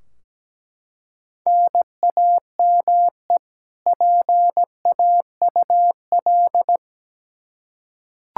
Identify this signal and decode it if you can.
Morse code, heard as a keyed tone, about 17 wpm, 700 Hz. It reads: NAME PAUL
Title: Practice QSO - Exchange 1 - Beginning - 17wpm